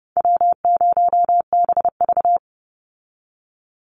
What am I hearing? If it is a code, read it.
Morse code, heard as a keyed tone, about 30 wpm, 700 Hz. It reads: W0BV
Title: Easy Call Signs - 30wpm